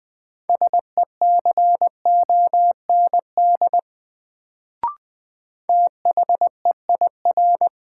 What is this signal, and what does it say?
Morse code, heard as a keyed tone, about 20 wpm, 700 Hz. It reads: SECOND THEIR
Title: Sentences from Top 300 Words - New Word Drill - 20wpm